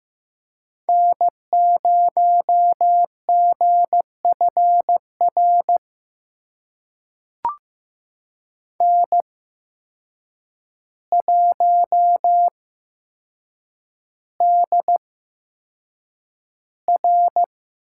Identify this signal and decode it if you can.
Morse code, heard as a keyed tone, about 15 wpm, spaced as for 7 wpm, 700 Hz. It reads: N0GFR N1DR
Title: USA Call Signs - 5 Characters - 8x Character Spacing - 15wpm